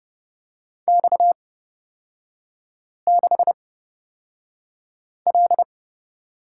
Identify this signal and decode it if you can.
Morse code, heard as a keyed tone, about 30 wpm, spaced as for 5 wpm, 700 Hz. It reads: X6L
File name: Lesson 37e - Letter x - Three Characters 30wpm